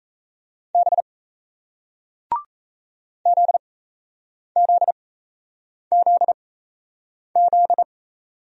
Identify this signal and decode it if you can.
Morse code, heard as a keyed tone, about 37 wpm, spaced as for 8 wpm, 700 Hz. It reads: B ZZZZ